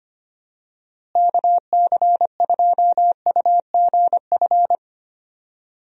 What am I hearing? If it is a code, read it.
Morse code, heard as a keyed tone, about 25 wpm, 700 Hz. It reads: KC2UGF
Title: USA Call Signs - 6 Characters - Repeated 2x - 25wpm